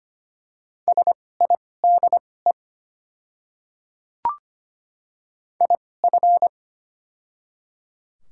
Morse code, heard as a keyed tone, about 25 wpm, spaced as for 18 wpm, 700 Hz. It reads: SIDE IF
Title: 2x Character Spacing - Top 100 Words - 25wpm